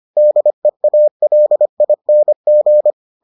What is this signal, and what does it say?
Morse code, heard as a keyed tone, about 25 wpm, 600 Hz. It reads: DEALING